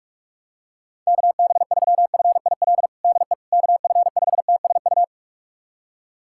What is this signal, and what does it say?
Morse code, heard as a keyed tone, about 46 wpm, 700 Hz. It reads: KB3FIL DE KF5TSU
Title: USA Callsign DE Callsign - 6 Characters - 45wpm